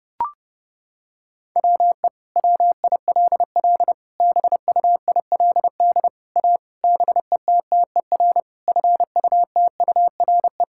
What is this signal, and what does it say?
Morse code, heard as a keyed tone, about 30 wpm, 700 Hz. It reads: WE WILL BUILD A BETTER FUTURE
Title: Sentences from Top 1000 Words - Review Part 6 - 30wpm